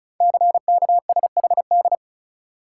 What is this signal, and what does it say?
Morse code, heard as a keyed tone, about 36 wpm, 700 Hz. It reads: CKSHD
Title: Code Groups - 5 Letters - 35wpm